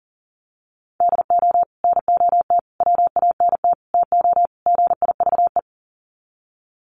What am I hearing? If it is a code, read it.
Morse code, heard as a keyed tone, about 40 wpm, 700 Hz. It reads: DO NOT WANT TO GIVE
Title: Sets of 5 Words Encore 40wpm